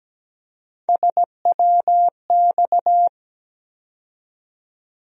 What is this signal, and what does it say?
Morse code, heard as a keyed tone, about 17 wpm, 700 Hz. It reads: SWX
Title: USA Call Sign Postfixes - 3 Letters 17wpm